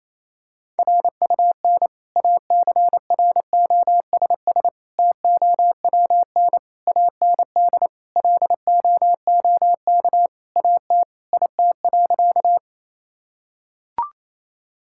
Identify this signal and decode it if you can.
Morse code, heard as a keyed tone, about 27 wpm, 700 Hz. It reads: RUN ACROSS TOWN AND LOOK AT IT.
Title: Sentences from Top 1000 Words - Review Part 5 - 28wpm